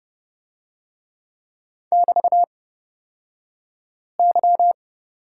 Morse code, heard as a keyed tone, about 30 wpm, spaced as for 5 wpm, 700 Hz. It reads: <BT>Y
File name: Lesson 36d - BT - Two Characters 30wpm